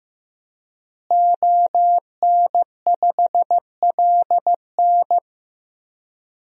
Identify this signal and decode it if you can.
Morse code, heard as a keyed tone, about 16 wpm, 700 Hz. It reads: ON5LN
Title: DXCC-100 Call Signs - 5 Characters - Repeated 2x - 15wpm